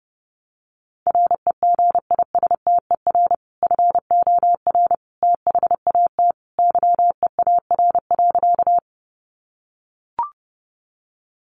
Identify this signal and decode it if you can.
Morse code, heard as a keyed tone, about 30 wpm, 700 Hz. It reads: REGISTER FOR THAT YEAR.
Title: Sentences from Top 1200 Words 30wpm